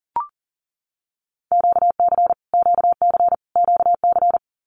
Morse code, heard as a keyed tone, about 38 wpm, 700 Hz. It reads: QC QC QC
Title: Canadian Provinces and Territories - Repeated 3x - 40wpm